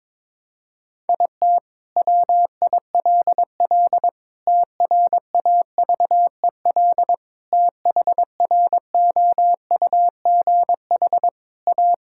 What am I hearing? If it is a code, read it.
Morse code, heard as a keyed tone, about 22 wpm, 700 Hz. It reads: IT WILL TRAVEL THROUGH A
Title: Sentences from Top 1000 Words - Review Part 6 - 22wpm